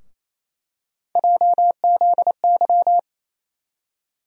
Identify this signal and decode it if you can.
Morse code, heard as a keyed tone, about 28 wpm, 700 Hz. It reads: JZY
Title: USA Call Sign Postfixes - 3 Letters 28wpm